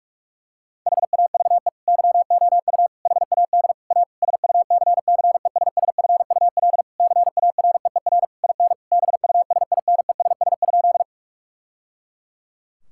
Morse code, heard as a keyed tone, about 41 wpm, 700 Hz. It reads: HAVE YOU HAD A SUCCESSFUL CAREER IN BUSINESS?
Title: Sentences from Top 1000 Words - Review Part 6 - 45wpm